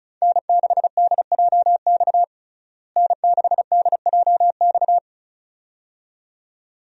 Morse code, heard as a keyed tone, about 36 wpm, 700 Hz. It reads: N6DJX N6DJX
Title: USA Call Signs - 5 Characters - Repeated 2x - 35wpm